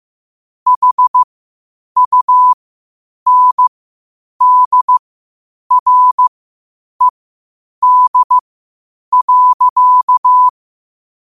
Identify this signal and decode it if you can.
Morse code, heard as a keyed tone, about 15 wpm, spaced as for 8 wpm, 1000 Hz. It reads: HUNDRED.
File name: qotdfiles / 8wpm